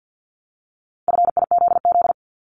Morse code, heard as a keyed tone, about 50 wpm, 700 Hz. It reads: FIZZ
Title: Multi-letter Phonemes Repeated 3x 50wpm